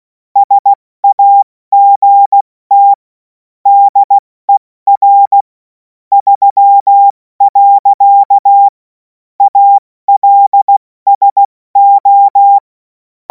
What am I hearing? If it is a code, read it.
Morse code, heard as a keyed tone, about 16 wpm, 800 Hz. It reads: SAGT DER 3. ALSO
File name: Morsekurs / Witze / Witz 11. August 2026 70 F80